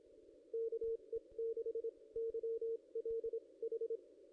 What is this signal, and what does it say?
Morse code, heard as a keyed tone, about 26 wpm, spaced as for 21 wpm, 440 Hz. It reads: KE6YLH
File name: contests / ss-cw-k5tr-op-n5rz -1600